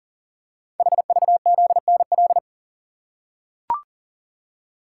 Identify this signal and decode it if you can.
Morse code, heard as a keyed tone, about 44 wpm, 700 Hz. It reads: HVZNL
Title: Code Groups - 5 Letters - 40wpm